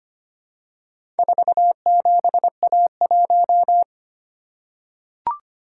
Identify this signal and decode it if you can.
Morse code, heard as a keyed tone, about 25 wpm, 700 Hz. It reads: 47A1
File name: practice / 25wpm / Four Letter-Number Combinations - 25wpm